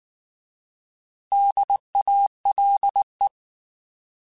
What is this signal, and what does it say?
Morse code, heard as a keyed tone, about 19 wpm, 800 Hz. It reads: DALE